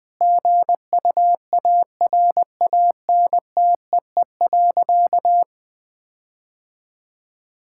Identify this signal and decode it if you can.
Morse code, heard as a keyed tone, about 20 wpm, 700 Hz. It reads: GUARANTEE.
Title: Sentences from Top 1600 Words 20wpm